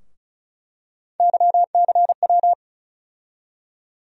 Morse code, heard as a keyed tone, about 36 wpm, 700 Hz. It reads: YCW
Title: USA Call Sign Postfixes - 3 Letters 35wpm